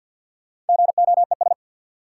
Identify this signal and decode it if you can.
Morse code, heard as a keyed tone, about 52 wpm, 700 Hz. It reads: GOES
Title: Sentences from Top 1000 Words - Review Part 5 - New Word Drill - 50wpm